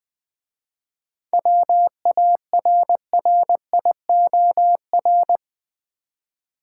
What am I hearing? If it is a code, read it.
Morse code, heard as a keyed tone, about 20 wpm, 700 Hz. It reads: WARRIOR
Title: 7 Letter Words - First 3 Letters Given - 20wpm